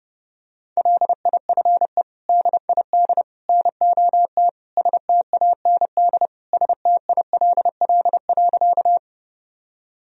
Morse code, heard as a keyed tone, about 30 wpm, 700 Hz. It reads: LIFE DID NOT STAND STILL.
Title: Sentences from Top 1000 Words - Review Part 1 - 30wpm